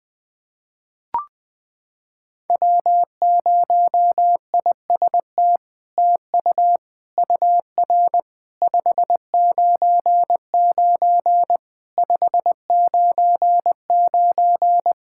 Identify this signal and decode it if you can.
Morse code, heard as a keyed tone, about 21 wpm, 700 Hz. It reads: W0IST TU UR 599 599
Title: SOTA QSO Practice 20wpm